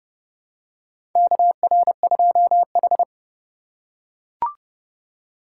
Morse code, heard as a keyed tone, about 32 wpm, 700 Hz. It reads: KR2H
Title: USA Call Signs - 4 Characters - 3x Character Spacing - 30wpm